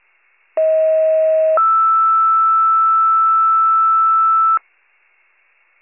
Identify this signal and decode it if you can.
Two-tone page, heard as a one-second tone, then a three-second tone, about 640 then 1320 Hz. It is heard from 0.6 to 4.6 s.